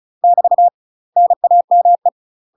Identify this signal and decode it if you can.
Morse code, heard as a keyed tone, about 34 wpm, 700 Hz. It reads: <BT> NAME
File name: UK MASTER QSOs TRAINING / 35WPM / UK MASTER QSOs 0034